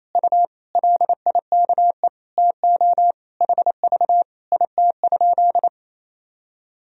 Morse code, heard as a keyed tone, about 28 wpm, 700 Hz. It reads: U LIKE TO HV IT?
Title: Sentences from Top 100 Words and Abbreviations 28wpm